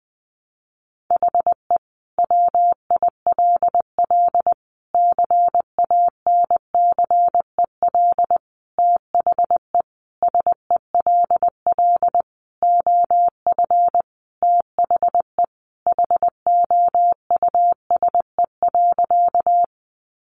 Morse code, heard as a keyed tone, about 20 wpm, 700 Hz. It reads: HE WILL CANCEL THE SELL OF THE HOUSE.
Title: Sentences from Top 1200 Words 20wpm